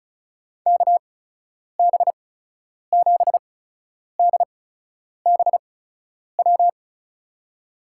Morse code, heard as a keyed tone, about 36 wpm, spaced as for 10 wpm, 700 Hz. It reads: KB7DBW
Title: USA Call Signs - 6 Characters - 8x Character Spacing - 35wpm